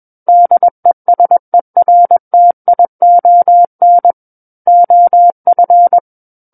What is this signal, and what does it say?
Morse code, heard as a keyed tone, about 21 wpm, 700 Hz. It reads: DESERTION OF
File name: Book One / 12-Chapter 12 - 21 WPM